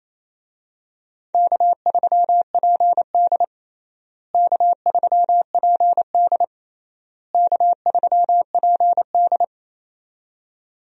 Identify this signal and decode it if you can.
Morse code, heard as a keyed tone, about 28 wpm, 700 Hz. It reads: K3PD K3PD K3PD
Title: USA Call Signs - 4 Characters - Repeated 3x - 28wpm